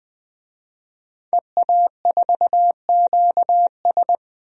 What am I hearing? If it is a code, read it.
Morse code, heard as a keyed tone, about 20 wpm, 700 Hz. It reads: EA4QS
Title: DXCC-100 Call Signs - 5 Characters - 20wpm